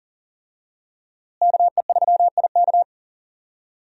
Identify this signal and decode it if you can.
Morse code, heard as a keyed tone, about 37 wpm, 700 Hz. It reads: KE3IK
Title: USA Call Signs - 5 Characters - Repeated 3x - 40wpm